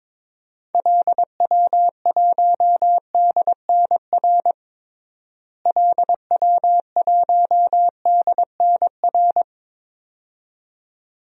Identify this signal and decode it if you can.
Morse code, heard as a keyed tone, about 22 wpm, 700 Hz. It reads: LW1DNR LW1DNR
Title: DXCC-100 Call Signs - 6 Characters - Repeated 2x - 22wpm